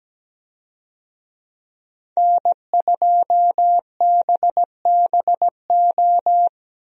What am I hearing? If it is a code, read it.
Morse code, heard as a keyed tone, about 17 wpm, 700 Hz. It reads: N2BBO
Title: USA Call Signs - 5 Characters - Repeated 2x - 17wpm